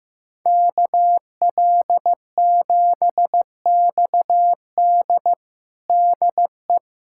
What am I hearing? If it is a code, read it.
Morse code, heard as a keyed tone, about 15 wpm, 700 Hz. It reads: KL7XD DE
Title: USA Callsign DE Callsign - 5 Characters - 15wpm